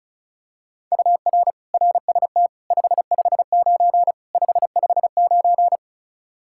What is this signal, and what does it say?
Morse code, heard as a keyed tone, about 34 wpm, 700 Hz. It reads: UR RST 559 559
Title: Practice QSO - Exchange 1 - Beginning - 35wpm